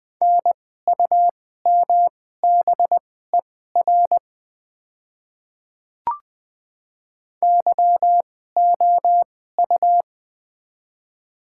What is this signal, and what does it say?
Morse code, heard as a keyed tone, about 21 wpm, spaced as for 15 wpm, 700 Hz. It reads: NUMBER YOU
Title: 2x Character Spacing - Top 100 Words - 20wpm